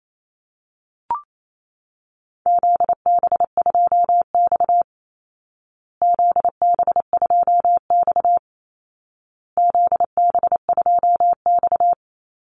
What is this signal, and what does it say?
Morse code, heard as a keyed tone, about 28 wpm, 700 Hz. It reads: ZB2X ZB2X ZB2X
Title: Call Signs Repeated 2x - 28wpm